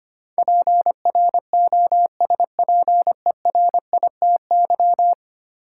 Morse code, heard as a keyed tone, about 25 wpm, 700 Hz. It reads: PROSPERITY
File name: 10 Letter Words - First 3 Letters Given - 25wpm